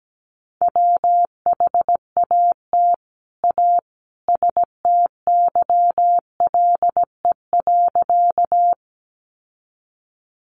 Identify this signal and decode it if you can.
Morse code, heard as a keyed tone, about 17 wpm, 700 Hz. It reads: WHAT A STYLE.
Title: Sentences from Top 800 Words 17wpm